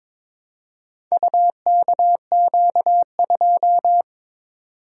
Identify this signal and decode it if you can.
Morse code, heard as a keyed tone, about 22 wpm, 700 Hz. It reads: UKQ2
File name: practice / 22wpm / Four Letter-Number Combinations - 22wpm